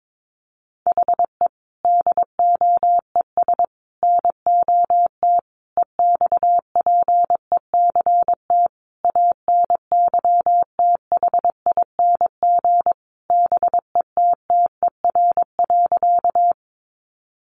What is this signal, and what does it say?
Morse code, heard as a keyed tone, about 22 wpm, 700 Hz. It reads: HE DOES NOT EXPECT ANYTHING BETTER.
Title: Sentences from Top 1100 Words 22wpm